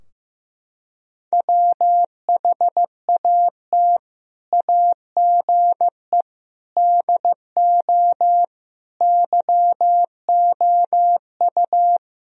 Morse code, heard as a keyed tone, about 15 wpm, 700 Hz. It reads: WHAT AGE DO YOU
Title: Sentences from Top 1000 Words - Review Part 3 - 15wpm